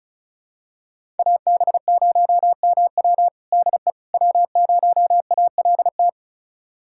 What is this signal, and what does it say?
Morse code, heard as a keyed tone, about 36 wpm, 700 Hz. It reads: AB0MW DE W0ALT
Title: USA Callsign DE Callsign - 5 Characters - 35wpm